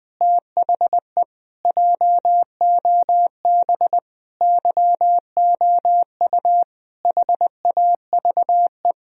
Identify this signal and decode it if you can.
Morse code, heard as a keyed tone, about 20 wpm, 700 Hz. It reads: THE JOB YOU HAVE
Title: Sentences from Top 1000 Words - Review Part 3 - 20wpm